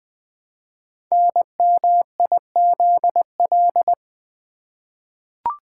Morse code, heard as a keyed tone, about 20 wpm, 700 Hz. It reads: NMIZL
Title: Code Groups - 5 Letters - 20wpm